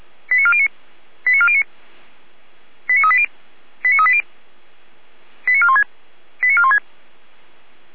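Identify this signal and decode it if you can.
Five-tone selective call (ZVEI), heard as quick runs of five tones, tones of about 70 ms. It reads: 89409 89409 89380 89380 89416 89416
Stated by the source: ZVEI five-tone selective call received by SDR.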